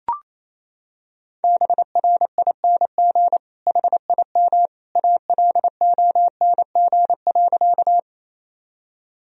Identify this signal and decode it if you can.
Morse code, heard as a keyed tone, about 28 wpm, 700 Hz. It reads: BRING HIM ALONG.
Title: Sentences from Top 1000 Words - Review Part 3 - 28wpm